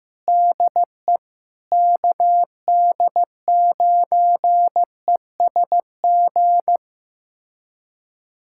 Morse code, heard as a keyed tone, about 15 wpm, 700 Hz. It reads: DE KD9ESG
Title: USA Callsign DE Callsign 15wpm